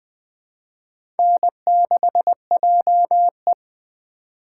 Morse code, heard as a keyed tone, about 21 wpm, 700 Hz. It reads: N6JE
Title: USA Call Signs - 4 Characters - 3x Character Spacing - 20wpm